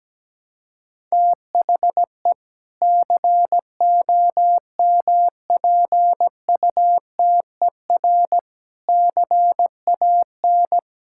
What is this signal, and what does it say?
Morse code, heard as a keyed tone, about 17 wpm, 700 Hz. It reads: THE COMPUTER CAN
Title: Sentences from Top 1000 Words - Review Part 6 - 17wpm